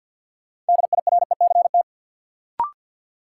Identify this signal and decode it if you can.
Morse code, heard as a keyed tone, about 50 wpm, 700 Hz. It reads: DIRECT
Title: Sentences from Top 400 Words - New Word Drill - 50wpm